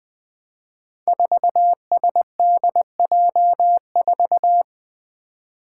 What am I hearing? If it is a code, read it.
Morse code, heard as a keyed tone, about 20 wpm, 700 Hz. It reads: 4SDJ4
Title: Five Challenging Letter-Number Combinations 20wpm